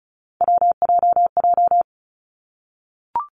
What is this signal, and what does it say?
Morse code, heard as a keyed tone, about 36 wpm, 700 Hz. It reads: WJJ
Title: Three Challenging Letter-Number Combinations 35wpm